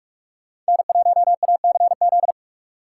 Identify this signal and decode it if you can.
Morse code, heard as a keyed tone, about 51 wpm, 700 Hz. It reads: N1ACZ
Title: Speed Racing - USA 1x3 Call Signs 45wpm